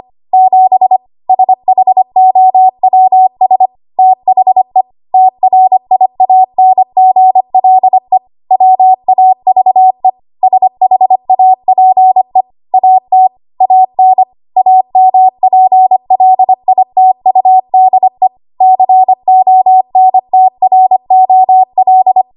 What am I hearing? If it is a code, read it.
Morse code, heard as a keyed tone, about 25 wpm, 750 Hz. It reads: 7 SHOWS THE TRIANGLE WAVE SHAPE AT AN AMPLITUDE CONTROL
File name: Archive / 25 WPM / 25WPM